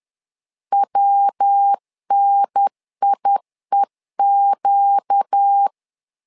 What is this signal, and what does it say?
Morse code, heard as a keyed tone, about 10 wpm, 800 Hz. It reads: WNIEQ
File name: LEZIONI / 4 Q 10wpm 2210